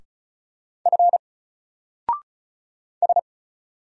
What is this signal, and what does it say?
Morse code, heard as a keyed tone, about 35 wpm, 700 Hz. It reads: F S
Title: Single Letter-Number 35wpm